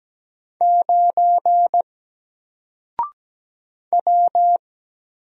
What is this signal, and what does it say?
Morse code, heard as a keyed tone, about 17 wpm, 700 Hz. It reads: 9 W
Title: Single Letter-Number Alt 17wpm